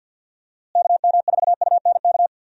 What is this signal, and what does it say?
Morse code, heard as a keyed tone, about 49 wpm, 700 Hz. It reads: KM4UNK
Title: USA Call Signs - 6 Characters - Repeated 2x - 50wpm